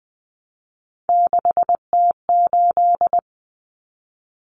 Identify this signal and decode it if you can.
Morse code, heard as a keyed tone, about 20 wpm, 700 Hz. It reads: 6T8
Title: Common Cut Numbers - 3-Digit 20wpm